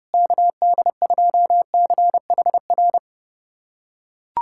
Morse code, heard as a keyed tone, about 32 wpm, 700 Hz. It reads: KD2CHR
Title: USA Callsign DE Callsign - 6 Characters - 30wpm